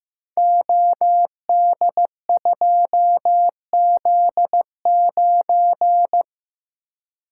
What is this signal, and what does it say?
Morse code, heard as a keyed tone, about 15 wpm, 700 Hz. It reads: OD2Z9
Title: Code Groups - 5 Letter:Numbers - 15wpm